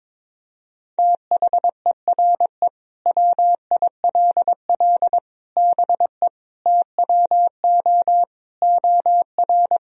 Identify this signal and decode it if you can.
Morse code, heard as a keyed tone, about 22 wpm, 700 Hz. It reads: THERE WILL BE TWO OR
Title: Sentences from Top 1000 Words - Review Part 3 - 22wpm